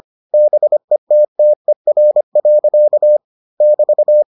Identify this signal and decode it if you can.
Morse code, heard as a keyed tone, about 25 wpm, 600 Hz. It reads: BETTER. <BT>